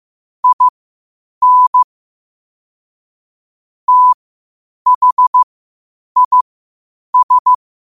Morse code, heard as a keyed tone, about 15 wpm, spaced as for 8 wpm, 1000 Hz. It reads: IN THIS
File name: qotdfiles / 8wpm